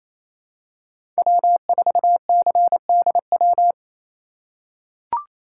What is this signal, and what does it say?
Morse code, heard as a keyed tone, about 28 wpm, 700 Hz. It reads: W4CDW
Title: USA Call Signs - 5 Characters - 4x Character Spacing - 28wpm